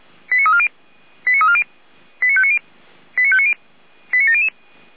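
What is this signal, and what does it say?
Five-tone selective call (ZVEI), heard as quick runs of five tones, tones of about 70 ms. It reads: 89340 89340 89590 89590 89700
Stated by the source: Received by SDR; ZVEI five-tone code 89304.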